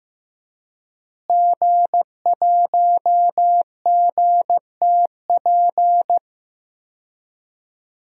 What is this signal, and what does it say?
Morse code, heard as a keyed tone, about 15 wpm, 700 Hz. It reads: G1GTP
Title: DXCC-100 Call Signs - 5 Characters - Repeated 3x - 15wpm